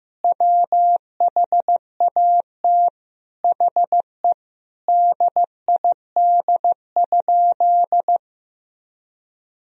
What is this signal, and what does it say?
Morse code, heard as a keyed tone, about 15 wpm, 700 Hz. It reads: WHAT HE DID?
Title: Sentences from Top 100 Words and Abbreviations 15wpm